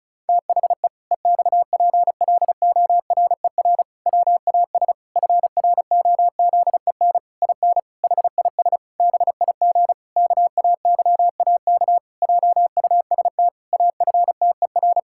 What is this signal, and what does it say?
Morse code, heard as a keyed tone, about 36 wpm, 700 Hz. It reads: THE EXPLORER WAS FROZEN IN HIS BIG KAYAK JUST AFTER
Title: Pangram Challenge v2 - 35wpm